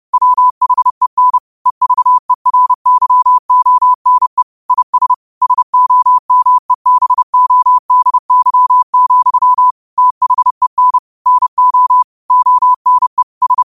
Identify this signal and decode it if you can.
Morse code, heard as a keyed tone, about 30 wpm, 1000 Hz. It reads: WHEN EVERYONE IS SOMEBODY, THEN NO ONE S
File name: qotdfiles / 30wpm